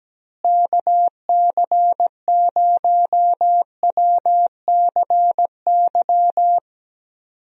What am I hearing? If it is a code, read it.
Morse code, heard as a keyed tone, about 17 wpm, 700 Hz. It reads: KC0WCY
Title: USA Call Signs - 6 Characters - Repeated 3x - 17wpm